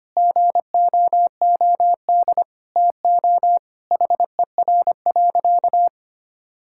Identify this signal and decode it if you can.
Morse code, heard as a keyed tone, about 25 wpm, 700 Hz. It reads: GOOD TO HER.
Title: Sentences from Top 1000 Words - Review Part 5 - 25wpm